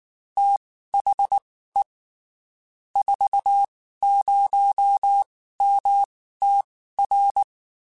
Morse code, heard as a keyed tone, about 19 wpm, spaced as for 14 wpm, 800 Hz. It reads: THE 40MTR
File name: (spaced) / QSOs Spaced-mp3s / kc7haa @18 sp